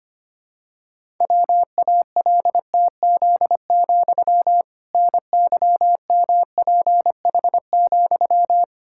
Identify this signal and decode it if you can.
Morse code, heard as a keyed tone, about 25 wpm, 700 Hz. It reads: WALTZ, NYMPH,
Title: Pangram Challenge v2 - 25wpm